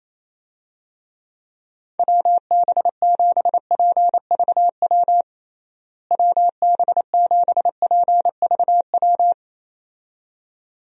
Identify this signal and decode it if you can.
Morse code, heard as a keyed tone, about 27 wpm, 700 Hz. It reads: WB7PVW WB7PVW
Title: USA Call Signs - 6 Characters - Repeated 2x - 28wpm